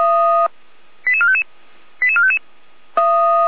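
Five-tone selective call (ZVEI), heard as quick runs of five tones, tones of about 70 ms. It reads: 88455 88455
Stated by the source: ZVEI five-tone selective call received by SDR, from the Feuerwehr Orlamünde fire brigade audio archive.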